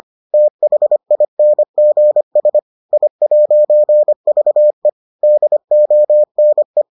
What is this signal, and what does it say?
Morse code, heard as a keyed tone, about 25 wpm, 600 Hz. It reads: THINGS I'VE DONE